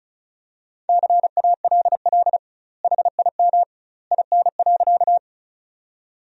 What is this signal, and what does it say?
Morse code, heard as a keyed tone, about 34 wpm, 700 Hz. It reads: CALL HIM IN.
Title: 2x Word-Spacing - Sentences from Top 100 Words - 35wpm